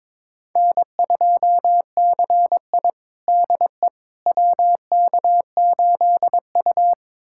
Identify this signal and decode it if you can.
Morse code, heard as a keyed tone, about 22 wpm, 700 Hz. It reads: N2CI DE WK8U
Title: USA Callsign DE Callsign - 4 Characters - 22wpm